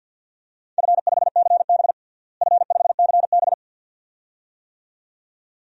Morse code, heard as a keyed tone, about 50 wpm, 700 Hz. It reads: F5CB F5CB
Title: DXCC-100 Call Signs - 4 Characters - Repeated 2x - 50wpm